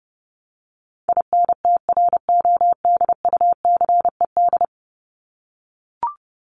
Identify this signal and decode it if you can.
Morse code, heard as a keyed tone, about 30 wpm, 700 Hz. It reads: INTRODUCED
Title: ED Suffix - Top 100 Words with Base Words 30wpm